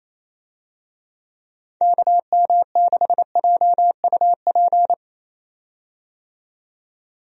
Morse code, heard as a keyed tone, about 28 wpm, 700 Hz. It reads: KM6JUP
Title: USA Call Signs - 6 Characters - 28wpm